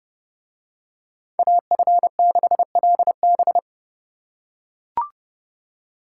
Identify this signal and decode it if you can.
Morse code, heard as a keyed tone, about 29 wpm, 700 Hz. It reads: AF6LB